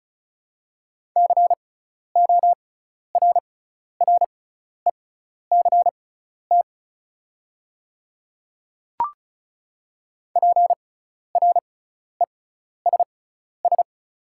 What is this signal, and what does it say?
Morse code, heard as a keyed tone, about 36 wpm, spaced as for 12 wpm, 700 Hz. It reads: CORRECT PRESS
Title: 6x Character Spacing - Top 500 Words - 35wpm